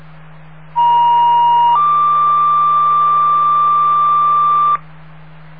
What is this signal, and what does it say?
Two-tone page, heard as a one-second tone, then a three-second tone, about 950 then 1180 Hz, with noise. It is heard from 0.7 to 4.8 s.